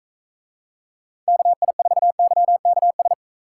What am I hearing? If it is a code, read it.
Morse code, heard as a keyed tone, about 45 wpm, 700 Hz. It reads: KI4YKS